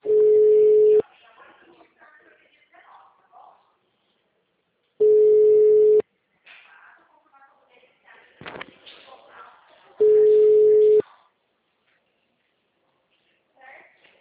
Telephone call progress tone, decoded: ringback tone

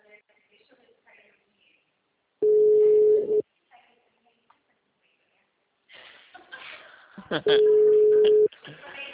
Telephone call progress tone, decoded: ringback tone

